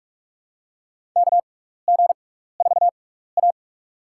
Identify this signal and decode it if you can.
Morse code, heard as a keyed tone, about 44 wpm, spaced as for 15 wpm, 700 Hz. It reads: KG4A
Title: USA Call Signs - 4 Characters - 6x Character Spacing - 45wpm